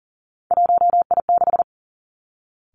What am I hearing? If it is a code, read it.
Morse code, heard as a keyed tone, about 38 wpm, 700 Hz. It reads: 1I6
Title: Three Challenging Letter-Number Combinations 40wpm